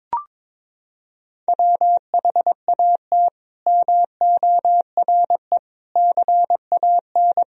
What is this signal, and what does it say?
Morse code, heard as a keyed tone, about 21 wpm, 700 Hz. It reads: WHAT MORE CAN
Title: Sentences from Top 1000 Words - Review Part 4 - 22wpm